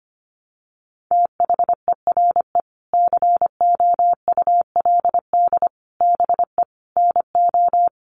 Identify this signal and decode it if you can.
Morse code, heard as a keyed tone, about 25 wpm, 700 Hz. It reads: THERE COULD BE NO
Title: Sentences from Top 600 Words 25wpm